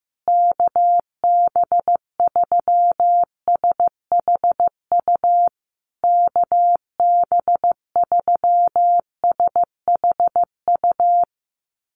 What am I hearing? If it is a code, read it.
Morse code, heard as a keyed tone, about 15 wpm, 700 Hz. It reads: KB3SHU KB3SHU
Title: USA Call Signs Repeated 3x 15wpm